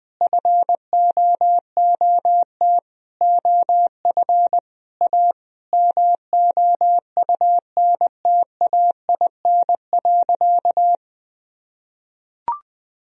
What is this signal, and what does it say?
Morse code, heard as a keyed tone, about 20 wpm, 700 Hz. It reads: FOOT OF A MOUNTAIN.
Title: Sentences from Top 1000 Words - Review Part 3 - 20wpm